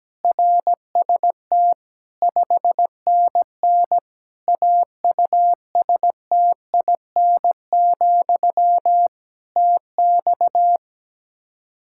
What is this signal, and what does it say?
Morse code, heard as a keyed tone, about 17 wpm, 700 Hz. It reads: RST 5NN AUSTIN, TX
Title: Practice QSO - Exchange 1 - Beginning - 17wpm